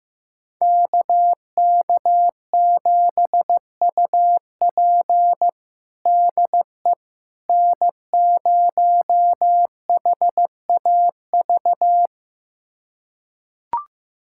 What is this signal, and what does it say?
Morse code, heard as a keyed tone, about 15 wpm, 700 Hz. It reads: KK7UP DE N0HAV
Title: USA Callsign DE Callsign - 5 Characters - 15wpm